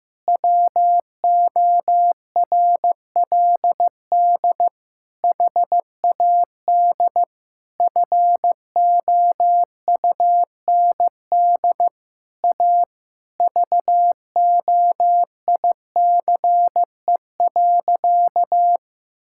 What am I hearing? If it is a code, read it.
Morse code, heard as a keyed tone, about 15 wpm, 700 Hz. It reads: WORLD HAD FOUND A VOICE.
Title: Sentences from Top 1600 Words 15wpm